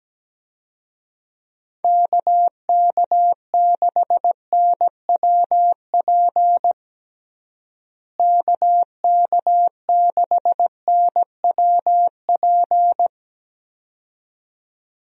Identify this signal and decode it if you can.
Morse code, heard as a keyed tone, about 17 wpm, 700 Hz. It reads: KK6NWP KK6NWP
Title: USA Call Signs - 6 Characters - Repeated 2x - 17wpm